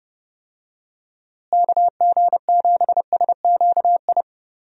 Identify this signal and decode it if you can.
Morse code, heard as a keyed tone, about 31 wpm, 700 Hz. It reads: KG7SQI
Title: USA Call Signs - 6 Characters - Repeated 3x - 30wpm